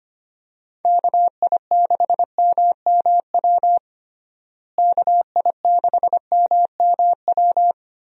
Morse code, heard as a keyed tone, about 25 wpm, 700 Hz. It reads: KI6MMW KI6MMW
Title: USA Call Signs - 6 Characters - Repeated 2x - 25wpm